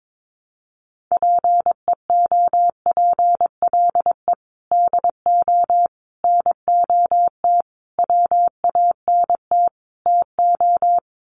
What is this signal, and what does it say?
Morse code, heard as a keyed tone, about 22 wpm, 700 Hz. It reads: PEOPLE DO NOT WANT TO
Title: Sets of 5 Words Encore2 22wpm